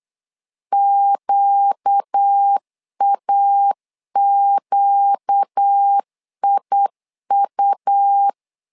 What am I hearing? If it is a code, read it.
Morse code, heard as a keyed tone, about 8 wpm, 800 Hz. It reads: QAQIU
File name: LEZIONI / 4 Q 8wpm 2732